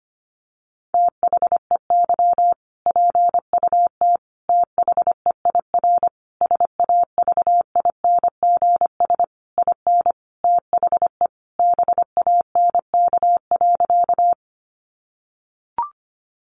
Morse code, heard as a keyed tone, about 25 wpm, 700 Hz. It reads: THEY PUT THEIR SAVINGS IN THE BANK.
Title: Sentences from Top 1200 Words 25wpm